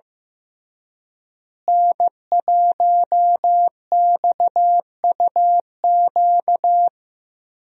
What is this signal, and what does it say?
Morse code, heard as a keyed tone, about 15 wpm, 700 Hz. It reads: N1XUQ
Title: USA Call Signs - 5 Characters - Repeated 3x - 15wpm